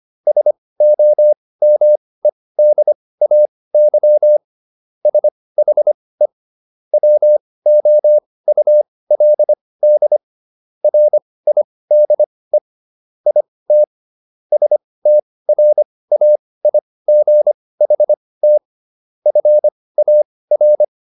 Morse code, heard as a keyed tone, about 25 wpm, spaced as for 18 wpm, 600 Hz. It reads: SOMEDAY SHE WOULD RIDE IT STRAIGHT FAR